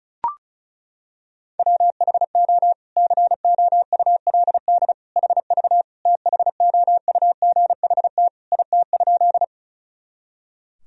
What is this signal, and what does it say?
Morse code, heard as a keyed tone, about 36 wpm, 700 Hz. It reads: WHO COULD HV THOUGHT IT?
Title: Sentences from Top 300 Words and Abbreviations 35wpm